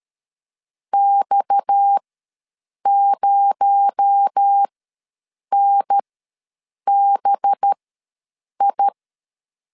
Morse code, heard as a keyed tone, about 13 wpm, spaced as for 7 wpm, 800 Hz. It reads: X0NBI
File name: LEZIONI / 7 GF9Z / 7 G ws 12wpm 3140